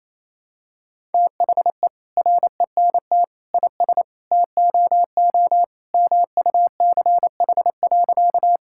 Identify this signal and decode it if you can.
Morse code, heard as a keyed tone, about 28 wpm, 700 Hz. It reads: THE RENT IS TOO MUCH.
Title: Sentences from Top 1000 Words - Review Part 5 - 28wpm